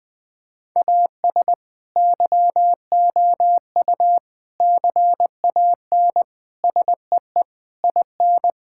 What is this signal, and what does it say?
Morse code, heard as a keyed tone, about 20 wpm, 700 Hz. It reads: AS YOU CAN SEE IN
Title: Sentences with 7-Word Phrases - 20wpm